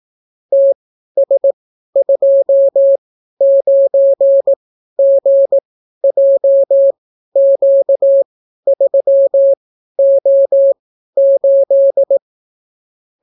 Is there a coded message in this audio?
Yes: Morse code from 0.5 to 12.2 s, heard as a keyed tone, about 18 wpm, spaced as for 12 wpm, 550 Hz. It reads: TS29GJQ3O8